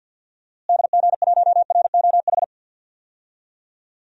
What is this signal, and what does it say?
Morse code, heard as a keyed tone, about 47 wpm, 700 Hz. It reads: DG1ROH
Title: DXCC-100 Call Signs - 6 Characters - Repeated 2x - 50wpm